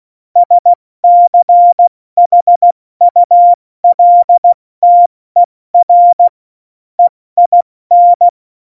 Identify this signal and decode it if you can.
Morse code, heard as a keyed tone, about 16 wpm, 700 Hz. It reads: SCHULTER EIN
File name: Morsekurs / Witze / Witz 04. September 2018, 70 F80